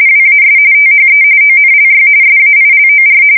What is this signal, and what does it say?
RTTY, about 45.45 baud. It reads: WAS THE ONLY SAILOR
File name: fsk170 / battleshipnj